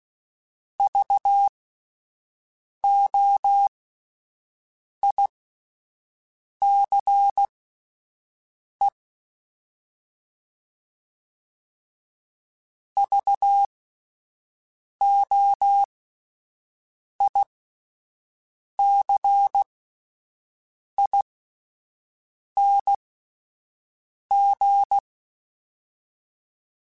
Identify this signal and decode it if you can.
Morse code, heard as a keyed tone, about 16 wpm, spaced as for 5 wpm, 800 Hz. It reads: VOICE VOICING